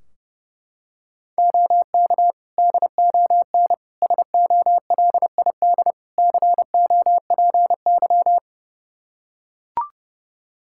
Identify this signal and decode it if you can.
Morse code, heard as a keyed tone, about 32 wpm, 700 Hz. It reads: OK DON SOLID COPY
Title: Practice QSO - Exchange 3 - More Info - 30wpm